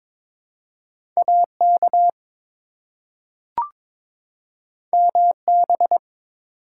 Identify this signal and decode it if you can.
Morse code, heard as a keyed tone, about 22 wpm, 700 Hz. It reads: AK MB
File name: ARRL Sections v2 - 22wpm